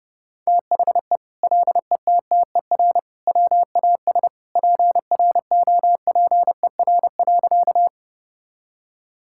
Morse code, heard as a keyed tone, about 30 wpm, 700 Hz. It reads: THE LETTER WAS PROPER.
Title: Sentences from Top 1000 Words - Review Part 5 - 30wpm